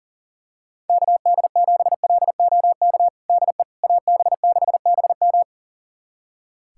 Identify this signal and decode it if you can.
Morse code, heard as a keyed tone, about 40 wpm, 700 Hz. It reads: KD7LOK DE AB6BM
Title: USA Callsign DE Callsign 40wpm